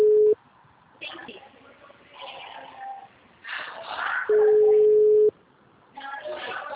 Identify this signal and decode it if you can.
Ringback tone (Europe, 425 Hz), with speech.